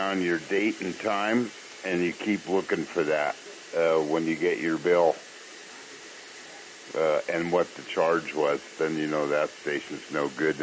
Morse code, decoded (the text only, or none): none